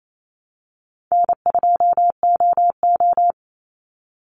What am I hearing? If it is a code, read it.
Morse code, heard as a keyed tone, about 28 wpm, 700 Hz. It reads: N2OO
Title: Call Signs - 28wpm